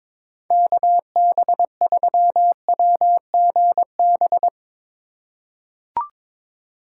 Morse code, heard as a keyed tone, about 21 wpm, 700 Hz. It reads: KB3WGB
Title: USA Call Signs - 3x Character Spacing - 22wpm